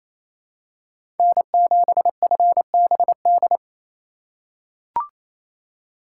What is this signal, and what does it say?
Morse code, heard as a keyed tone, about 27 wpm, 700 Hz. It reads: N7FBD
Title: USA Call Signs - 5 Characters - 6x Character Spacing - 28wpm